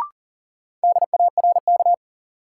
Morse code, heard as a keyed tone, about 40 wpm, 700 Hz. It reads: DARK
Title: Sentences from Top 500 Words - New Word Drill - 40wpm